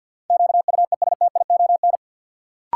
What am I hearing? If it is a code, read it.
Morse code, heard as a keyed tone, about 52 wpm, 700 Hz. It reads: QUESTION
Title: Sets of 5 Words Encore - New Word Drill - 50wpm